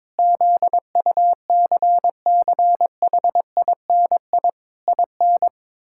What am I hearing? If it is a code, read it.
Morse code, heard as a keyed tone, about 21 wpm, 700 Hz. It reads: ZUCCHINI IN
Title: Pangram Challenge v2 - 22wpm